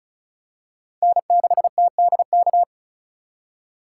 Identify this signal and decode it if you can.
Morse code, heard as a keyed tone, about 34 wpm, 700 Hz. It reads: N6TDK
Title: USA Call Signs - 5 Characters - 3x Character Spacing - 35wpm